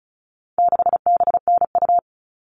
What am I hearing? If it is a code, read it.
Morse code, heard as a keyed tone, about 34 wpm, 700 Hz. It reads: 6BNU
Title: All Cut Numbers - 4-Digit 35wpm